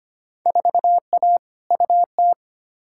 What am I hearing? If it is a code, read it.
Morse code, heard as a keyed tone, about 25 wpm, 700 Hz. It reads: 4A UT
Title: Field Day Exchanges - 25wpm